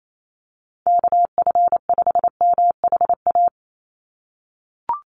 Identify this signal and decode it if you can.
Morse code, heard as a keyed tone, about 28 wpm, 700 Hz. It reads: KF5MHA
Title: USA Call Signs 28wpm